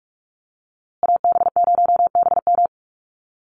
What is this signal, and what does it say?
Morse code, heard as a keyed tone, about 44 wpm, 700 Hz. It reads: AB0BM
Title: USA Call Signs Repeated 2x 45wpm